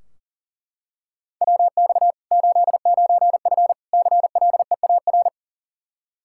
Morse code, heard as a keyed tone, about 40 wpm, 700 Hz. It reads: WX 89F CLEAR
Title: Practice QSO - Exchange 2 - Info - 40wpm